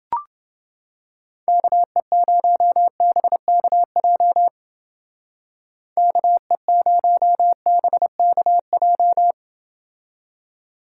Morse code, heard as a keyed tone, about 28 wpm, 700 Hz. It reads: KE0BKJ KE0BKJ